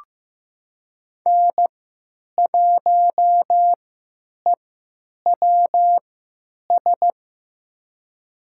Morse code, heard as a keyed tone, about 15 wpm, spaced as for 9 wpm, 700 Hz. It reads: N1EWS
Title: USA Call Signs - 5 Characters - 3x Character Spacing - 15wpm